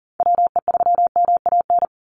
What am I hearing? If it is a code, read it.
Morse code, heard as a keyed tone, about 38 wpm, 690 Hz. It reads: WE3MAN
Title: USA Call Signs 40wpm